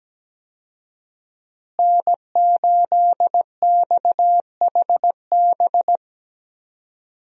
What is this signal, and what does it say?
Morse code, heard as a keyed tone, about 17 wpm, 700 Hz. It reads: N8XHB
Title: USA Call Signs - 5 Characters - Repeated 2x - 17wpm